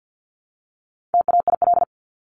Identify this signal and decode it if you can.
Morse code, heard as a keyed tone, about 50 wpm, 700 Hz. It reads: TAIL
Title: Top 500 Words 50wpm